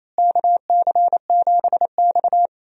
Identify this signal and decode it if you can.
Morse code, heard as a keyed tone, about 28 wpm, 700 Hz. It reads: KC7X
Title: USA Call Signs - 4 Characters - Repeated 3x - 28wpm